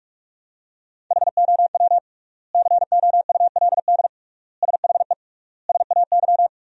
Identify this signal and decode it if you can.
Morse code, heard as a keyed tone, about 41 wpm, 700 Hz. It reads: HOW COULD SHE SAY
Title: 3x Word-Spacing - Sentences from Top 200 Words - 45wpm